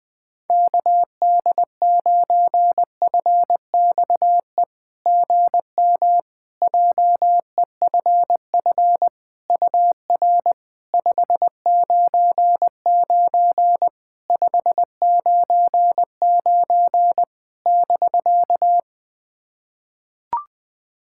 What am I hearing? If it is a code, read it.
Morse code, heard as a keyed tone, about 20 wpm, 690 Hz. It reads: KD9FXE GM JEFF UR 599 599 <BK>
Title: SOTA QSO Practice 20wpm